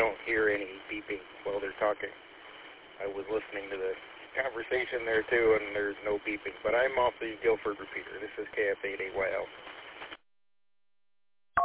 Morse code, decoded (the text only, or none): none